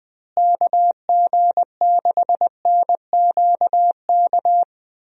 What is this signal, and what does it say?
Morse code, heard as a keyed tone, about 20 wpm, 700 Hz. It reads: KG6NQK
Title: USA Call Signs - 6 Characters - Repeated 2x - 20wpm